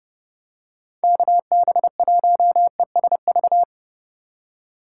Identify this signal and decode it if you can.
Morse code, heard as a keyed tone, about 31 wpm, 700 Hz. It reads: KB1ESV